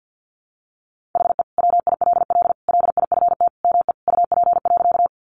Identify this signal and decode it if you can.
Morse code, heard as a keyed tone, about 50 wpm, 700 Hz. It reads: HE WILL LIFT ME UP.
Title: Sentences from Top 1000 Words 50wpm